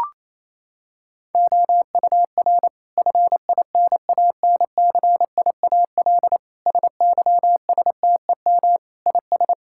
Morse code, heard as a keyed tone, about 28 wpm, 700 Hz. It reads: OUR FINANCIAL SYSTEM IS
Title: Sentences from Top 1000 Words - Review Part 5 - 28wpm